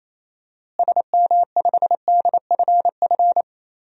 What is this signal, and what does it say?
Morse code, heard as a keyed tone, about 27 wpm, 700 Hz. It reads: SM5DFF
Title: DXCC-100 Call Signs - 6 Characters - Repeated 2x - 28wpm